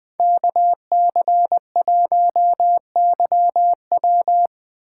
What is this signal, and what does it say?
Morse code, heard as a keyed tone, about 21 wpm, 700 Hz. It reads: KC1YW